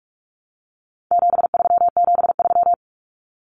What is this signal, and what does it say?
Morse code, heard as a keyed tone, about 44 wpm, 700 Hz. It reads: 7373
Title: Common Cut Numbers - 4-Digit 45wpm